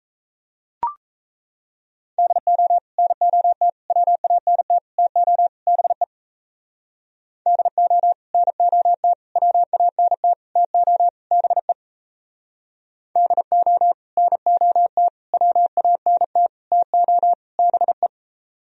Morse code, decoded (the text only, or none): DO NOT WANT TO BE DO NOT WANT TO BE DO NOT WANT TO BE